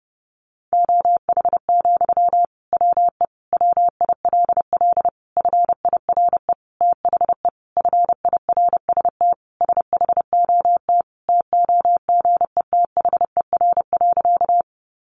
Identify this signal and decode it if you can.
Morse code, heard as a keyed tone, about 30 wpm, 700 Hz. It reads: OH, WE WILL FIRE THE FIRST SHOT TOGETHER.
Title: Sentences from Top 1100 Words 30wpm